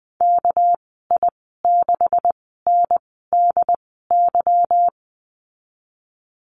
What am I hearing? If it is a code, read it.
Morse code, heard as a keyed tone, about 20 wpm, spaced as for 15 wpm, 700 Hz. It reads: KI6NDY
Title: 245 Lesson 39g - Letter z - Call Signs 20wpm